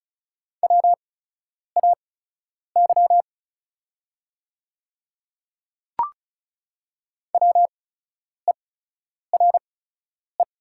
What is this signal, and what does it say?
Morse code, heard as a keyed tone, about 36 wpm, spaced as for 10 wpm, 700 Hz. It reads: WAY WERE
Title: 8x Character Spacing - Top 100 Words - 35wpm